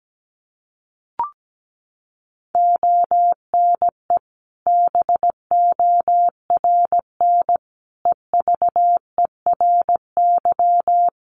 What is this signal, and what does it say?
Morse code, heard as a keyed tone, about 17 wpm, 700 Hz. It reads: ONE BORN EVERY
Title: Sentences from Top 1400 Words 17wpm